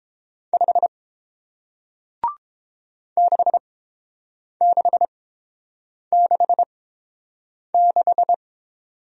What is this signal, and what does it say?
Morse code, heard as a keyed tone, about 31 wpm, spaced as for 7 wpm, 700 Hz. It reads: 5 6666